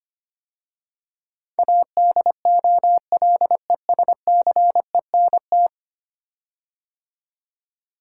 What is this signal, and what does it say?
Morse code, heard as a keyed tone, about 25 wpm, 700 Hz. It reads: ADOLESCENT
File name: 10 Letter Words - First 3 Letters Given - 25wpm